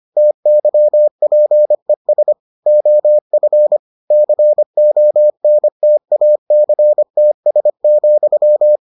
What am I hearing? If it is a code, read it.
Morse code, heard as a keyed tone, about 25 wpm, 600 Hz. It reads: TYPES OF CONTACTS,